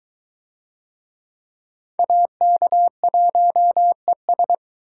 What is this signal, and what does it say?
Morse code, heard as a keyed tone, about 23 wpm, 700 Hz. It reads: AK1ES